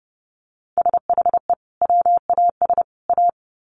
Morse code, heard as a keyed tone, about 30 wpm, 700 Hz. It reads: SHE WAS A
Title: Sets of 3 Words 30wpm